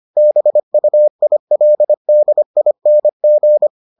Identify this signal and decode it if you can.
Morse code, heard as a keyed tone, about 25 wpm, 600 Hz. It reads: BUILDING